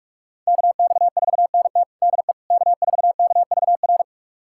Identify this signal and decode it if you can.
Morse code, heard as a keyed tone, about 46 wpm, 700 Hz. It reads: KX4NT DE K4KVR
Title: USA Callsign DE Callsign - 5 Characters - 45wpm